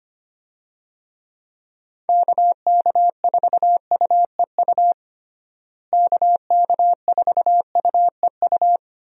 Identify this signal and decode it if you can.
Morse code, heard as a keyed tone, about 25 wpm, 700 Hz. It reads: KK4UEU KK4UEU
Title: USA Call Signs - 6 Characters - Repeated 2x - 25wpm